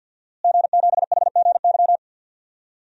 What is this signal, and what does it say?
Morse code, heard as a keyed tone, about 51 wpm, 700 Hz. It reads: G7HGY
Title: DXCC-100 Call Signs - 5 Characters - Repeated 2x - 50wpm